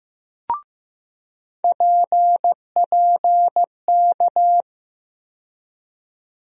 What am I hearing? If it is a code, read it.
Morse code, heard as a keyed tone, about 15 wpm, 700 Hz. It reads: PPK
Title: USA Call Sign Postfixes - 3 Letters 15wpm